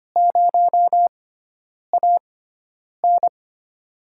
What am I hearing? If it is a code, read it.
Morse code, heard as a keyed tone, about 25 wpm, spaced as for 9 wpm, 700 Hz. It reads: 0AN
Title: USA Call Signs - 5 Characters - 6x Character Spacing - 25wpm